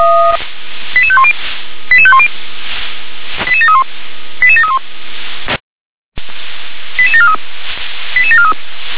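Five-tone selective call (ZVEI), heard as quick runs of five tones, tones of about 70 ms. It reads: 88510 88510 88731 88731 88743 88743
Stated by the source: ZVEI five-tone selective call 88743 received on an SDR.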